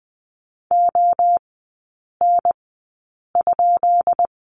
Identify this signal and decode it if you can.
Morse code, heard as a keyed tone, about 20 wpm, spaced as for 8 wpm, 690 Hz. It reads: ON?
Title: 212 Lesson 34e - Number 8 - Three Characters 20wpm